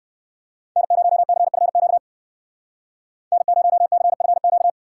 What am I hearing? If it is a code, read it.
Morse code, heard as a keyed tone, about 68 wpm, 700 Hz. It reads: N0CFQ N0CFQ